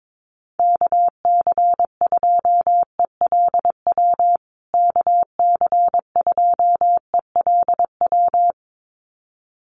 Morse code, heard as a keyed tone, about 22 wpm, 700 Hz. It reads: KC2ELW KC2ELW
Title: USA Call Signs Repeated 2x 22wpm